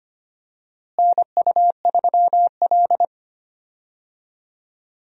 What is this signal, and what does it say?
Morse code, heard as a keyed tone, about 25 wpm, 700 Hz. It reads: NU3L
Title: USA Call Signs - 4 Characters - Repeated 3x - 25wpm